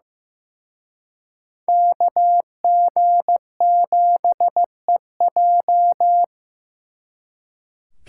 Morse code, heard as a keyed tone, about 18 wpm, 700 Hz. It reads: KG7EJ
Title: Speed Racing - USA 2x2 Call Signs 15wpm